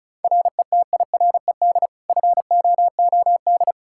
Morse code, heard as a keyed tone, about 36 wpm, 700 Hz. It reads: RETIRED FOOD
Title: Practice QSO - Exchange 3 - More Info - 35wpm